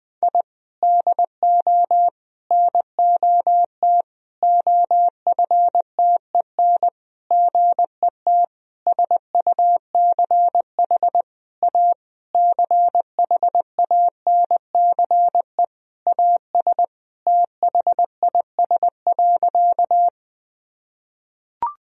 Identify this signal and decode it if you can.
Morse code, heard as a keyed tone, about 20 wpm, 700 Hz. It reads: I DO NOT OFTEN GET SUCH A CHANCE AS THIS.
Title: Sentences from Top 1000 Words - Review Part 1 - 20wpm